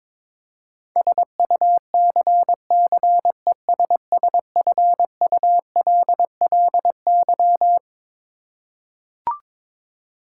Morse code, heard as a keyed tone, about 22 wpm, 700 Hz. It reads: SUCCESSFULLY
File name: practice / 22wpm / Twelve Letter Words - 22wpm